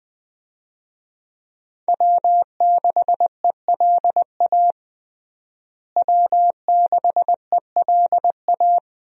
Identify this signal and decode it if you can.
Morse code, heard as a keyed tone, about 21 wpm, 700 Hz. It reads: W6ELA W6ELA
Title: USA Call Signs - 5 Characters - Repeated 3x - 20wpm